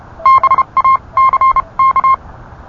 Morse code, heard as a keyed tone, about 31 wpm, spaced as for 25 wpm, 1060 Hz. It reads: BACK